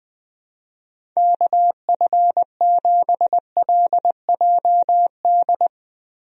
Morse code, heard as a keyed tone, about 20 wpm, 700 Hz. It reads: KF7LJD
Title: USA Call Signs - 6 Characters - Repeated 2x - 20wpm